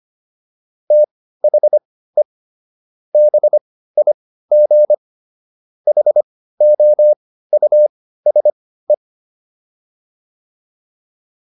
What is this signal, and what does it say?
Morse code, heard as a keyed tone, about 25 wpm, spaced as for 15 wpm, 600 Hz. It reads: THE BIG HOUSE